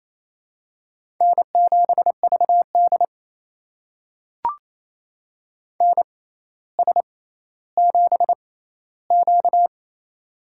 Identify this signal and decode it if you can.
Morse code, heard as a keyed tone, about 28 wpm, spaced as for 14 wpm, 700 Hz. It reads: N7VD NS7Q
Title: USA Call Signs - 4 Characters - 6x Character Spacing - 28wpm